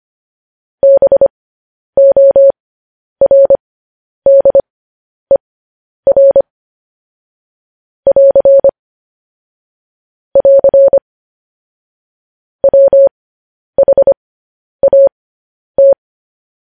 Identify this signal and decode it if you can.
Morse code, heard as a keyed tone, about 25 wpm, spaced as for 10 wpm, 550 Hz. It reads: BORDER <AR> <AR> WHAT